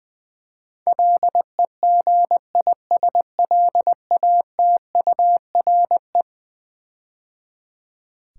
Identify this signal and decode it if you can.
Morse code, heard as a keyed tone, about 20 wpm, 700 Hz. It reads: LEGISLATURE
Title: Build A Word - 4 Letters At A Time - 11-Letter Words - 20wpm